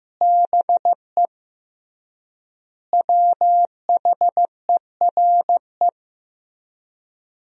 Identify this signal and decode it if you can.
Morse code, heard as a keyed tone, about 16 wpm, 700 Hz. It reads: BE WHERE
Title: 3x Word-Spacing - Sentences from Top 200 Words - 15wpm